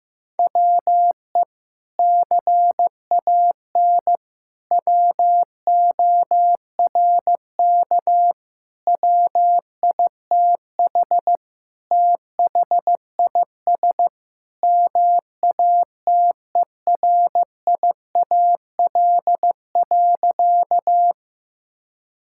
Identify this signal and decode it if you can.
Morse code, heard as a keyed tone, about 15 wpm, 700 Hz. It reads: WE CAN WORK WITH THIS MATERIAL.
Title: Sentences from Top 1000 Words - Review Part 5 - 15wpm